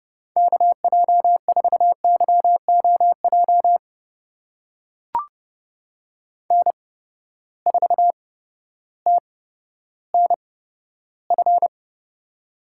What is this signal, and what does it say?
Morse code, heard as a keyed tone, about 30 wpm, spaced as for 14 wpm, 700 Hz. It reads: KJ4YOJ N4TNF
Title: USA Call Signs - 8x Character Spacing - 30wpm